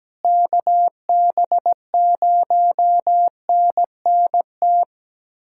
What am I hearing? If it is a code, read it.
Morse code, heard as a keyed tone, about 17 wpm, 700 Hz. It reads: KB0NNT
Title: USA Callsign DE Callsign - 6 Characters - 17wpm